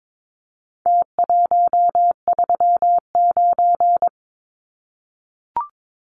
Morse code, heard as a keyed tone, about 22 wpm, 700 Hz. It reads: T139
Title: Numbers - Contest Style - 4-Digit 22wpm